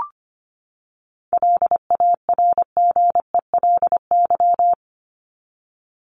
Morse code, heard as a keyed tone, about 25 wpm, 700 Hz. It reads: LARGELY
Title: LY Suffix - Top 100 Words with Base Words 25wpm